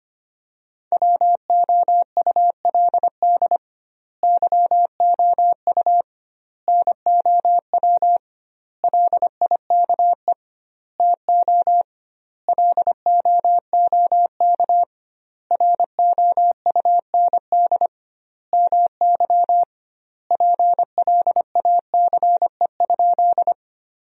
Morse code, heard as a keyed tone, about 25 wpm, 700 Hz. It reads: WOULD YOU NOW LIKE TO LOOK ROUND MY PLACE?
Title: 2x Word-Spacing - Sentences from Top 200 Words - 25wpm